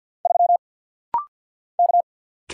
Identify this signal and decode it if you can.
Morse code, heard as a keyed tone, about 49 wpm, 700 Hz. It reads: 3 K
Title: Single Letter-Number - Rapid-Fire 50wpm